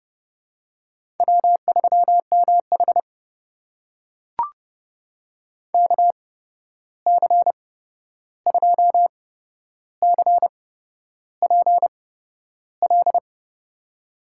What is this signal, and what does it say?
Morse code, heard as a keyed tone, about 30 wpm, spaced as for 11 wpm, 700 Hz. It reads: W3MH KC2CPL
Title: USA Call Signs - 8x Character Spacing - 30wpm